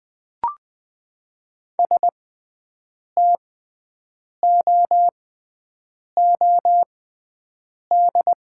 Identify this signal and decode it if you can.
Morse code, heard as a keyed tone, about 20 wpm, spaced as for 7 wpm, 700 Hz. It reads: STOOD
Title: 6x Character Spacing - Top 500 Words - 20wpm